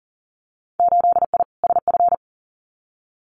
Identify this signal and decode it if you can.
Morse code, heard as a keyed tone, about 40 wpm, 700 Hz. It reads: 8I SF
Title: Winter Field Day Exchanges Encore 40wpm